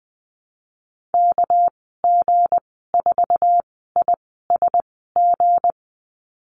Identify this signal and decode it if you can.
Morse code, heard as a keyed tone, about 20 wpm, spaced as for 15 wpm, 700 Hz. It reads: KG4ISG
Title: 231 Lesson 37g - Letter x - Call Signs 20wpm